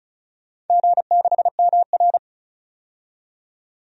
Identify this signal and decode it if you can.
Morse code, heard as a keyed tone, about 36 wpm, 700 Hz. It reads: G6MR
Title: DXCC-100 Call Signs - 4 Characters - 35wpm